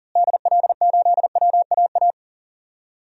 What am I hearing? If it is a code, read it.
Morse code, heard as a keyed tone, about 40 wpm, 700 Hz. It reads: DL8WAA
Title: DXCC-100 Call Signs - 6 Characters - Repeated 2x - 40wpm